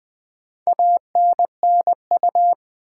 Morse code, heard as a keyed tone, about 20 wpm, 700 Hz. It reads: ANNU
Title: Build A Word - 2 Letters At A Time - 6-Letter Words - 20wpm